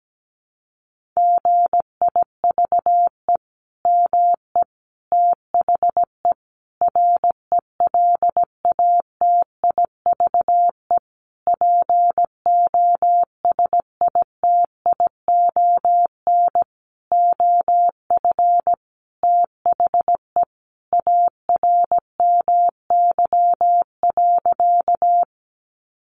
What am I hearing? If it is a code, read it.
Morse code, heard as a keyed tone, about 17 wpm, 700 Hz. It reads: GIVE ME THE RELATIVE POSITION OF THE ARMY.
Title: Sentences from Top 1400 Words 17wpm